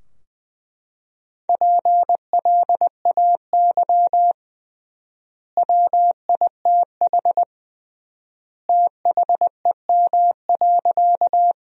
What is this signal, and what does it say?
Morse code, heard as a keyed tone, about 20 wpm, 700 Hz. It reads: PLAY WITH THEM.
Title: 3x Word-Spacing - Sentences from Top 200 Words - 20wpm